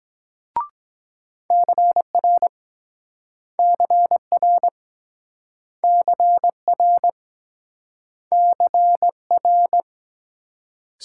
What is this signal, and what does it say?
Morse code, heard as a keyed tone, about 24 wpm, 700 Hz. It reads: CR CR CR CR